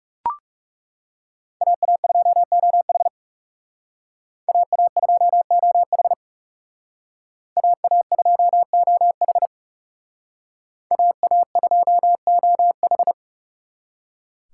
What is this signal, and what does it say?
Morse code, heard as a keyed tone, about 36 wpm, 700 Hz. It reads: AA2OH AA2OH AA2OH AA2OH
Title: Speed Racing - USA 2x2 Call Signs 30wpm